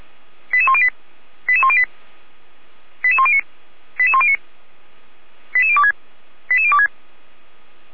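Five-tone selective call (ZVEI), heard as quick runs of five tones, tones of about 70 ms. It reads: 88108 88108 88109 88109 88026 88026
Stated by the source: Five-tone ZVEI selective call received on an SDR, from the fire brigade Feuerwehr Orlamünde.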